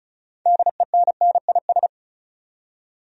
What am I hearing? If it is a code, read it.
Morse code, heard as a keyed tone, about 34 wpm, 700 Hz. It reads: DENNIS
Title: USA Common Names - 35wpm